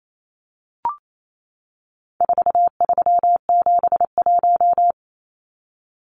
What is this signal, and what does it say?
Morse code, heard as a keyed tone, about 28 wpm, 700 Hz. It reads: 4371
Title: Numbers - 4-Digit 28wpm